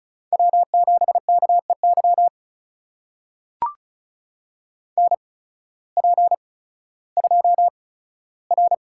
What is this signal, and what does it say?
Morse code, heard as a keyed tone, about 36 wpm, spaced as for 16 wpm, 700 Hz. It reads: W7KEY NP2R
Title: USA Call Signs - 5 Characters - 8x Character Spacing - 35wpm